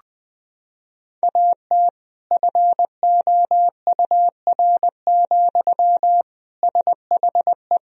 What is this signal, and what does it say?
Morse code, heard as a keyed tone, about 21 wpm, 700 Hz. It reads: AT FOUR, SHE
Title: Sentences from Top 1000 Words - Review Part 1 - 20wpm